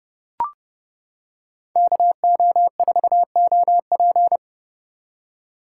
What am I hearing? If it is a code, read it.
Morse code, heard as a keyed tone, about 30 wpm, 700 Hz. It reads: KO4OP